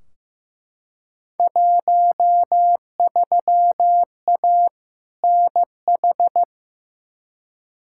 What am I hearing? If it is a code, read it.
Morse code, heard as a keyed tone, about 15 wpm, 700 Hz. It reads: 13A NH
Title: Field Day Exchanges Encore - 15wpm